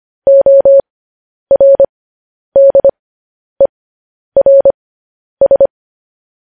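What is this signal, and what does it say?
Morse code, heard as a keyed tone, about 25 wpm, spaced as for 10 wpm, 550 Hz. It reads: ORDERS
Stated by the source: Morse code news headlines practice at 10 wpm.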